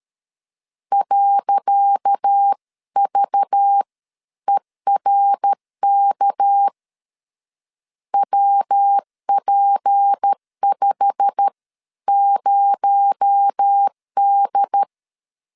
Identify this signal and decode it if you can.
Morse code, heard as a keyed tone, about 13 wpm, spaced as for 11 wpm, 800 Hz. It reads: .VERK WP50D
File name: LEZIONI / 10 SIGNS / 10C DOT 12wpm 2451